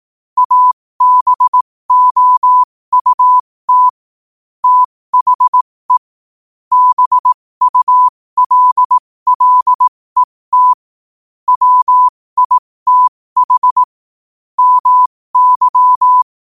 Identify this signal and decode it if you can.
Morse code, heard as a keyed tone, about 18 wpm, spaced as for 15 wpm, 1000 Hz. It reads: ABOUT THE BULLET WITH MY
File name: qotdfiles / 15wpm